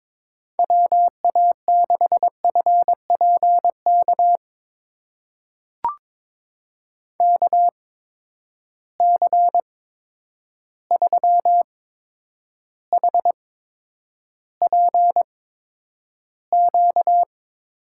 Morse code, heard as a keyed tone, about 22 wpm, spaced as for 9 wpm, 700 Hz. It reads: WA6FPK KC3HPQ
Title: USA Call Signs - 6 Characters - 8x Character Spacing - 22wpm